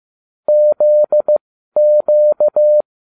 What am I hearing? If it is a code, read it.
Morse code, heard as a keyed tone, about 15 wpm, spaced as for 12 wpm, 600 Hz. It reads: ZQ